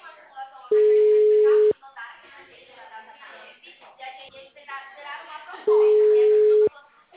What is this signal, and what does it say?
Ringback tone (Europe, 425 Hz), with speech.